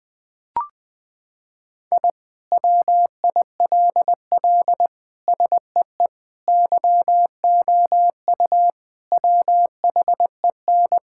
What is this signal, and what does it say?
Morse code, heard as a keyed tone, about 20 wpm, 700 Hz. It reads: I WILL SEE YOU WHEN
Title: Sentences with 2-Word Phrases - 20wpm